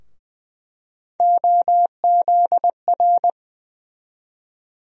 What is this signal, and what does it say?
Morse code, heard as a keyed tone, about 20 wpm, 700 Hz. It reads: OZR
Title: USA Call Sign Postfixes - 3 Letters 20wpm